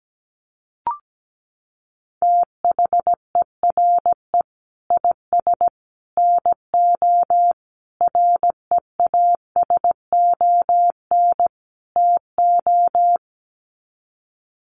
Morse code, heard as a keyed tone, about 17 wpm, 700 Hz. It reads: THERE IS NO REASON TO
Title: Sets of 5 Words Encore 17wpm